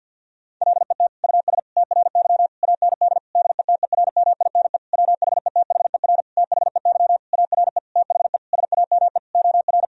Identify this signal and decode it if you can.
Morse code, heard as a keyed tone, about 51 wpm, 700 Hz. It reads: LET US TRY AND DETERMINE WHETHER THEY ARE THE SAME OR
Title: Sentences from Top 1000 Words - Review Part 4 - 50wpm